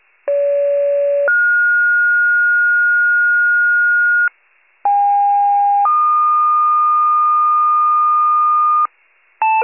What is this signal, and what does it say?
Two-tone page, heard as a one-second tone, then a three-second tone, about 570 then 1430 Hz, about 810 then 1220 Hz, with noise. It is heard from 0.3 to 8.9 s.